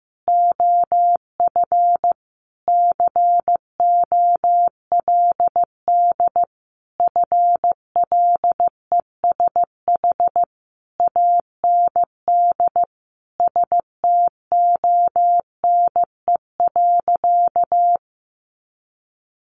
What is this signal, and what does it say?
Morse code, heard as a keyed tone, about 15 wpm, 700 Hz. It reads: OF COLD FLESH AND STONE.
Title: A Sound Of Thunder 15wpm